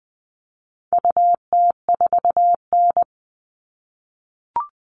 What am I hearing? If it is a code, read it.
Morse code, heard as a keyed tone, about 20 wpm, 700 Hz. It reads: UT4N
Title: All Cut Numbers - 4-Digit 20wpm